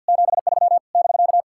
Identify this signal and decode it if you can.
Morse code, heard as a keyed tone, about 59 wpm, 700 Hz. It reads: 73 <BK>
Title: Speed-Racing Common QSO Phrases - 50wpm